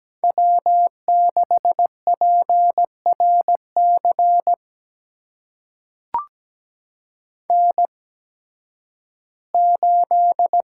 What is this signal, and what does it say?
Morse code, heard as a keyed tone, about 17 wpm, spaced as for 11 wpm, 700 Hz. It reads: W6PRC N8
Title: USA Call Signs - 5 Characters - 8x Character Spacing - 17wpm